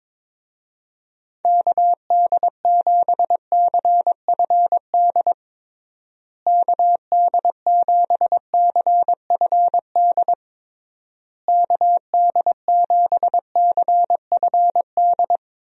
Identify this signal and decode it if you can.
Morse code, heard as a keyed tone, about 22 wpm, 700 Hz. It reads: KD7CFD KD7CFD KD7CFD
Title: USA Call Signs - 6 Characters - Repeated 3x - 22wpm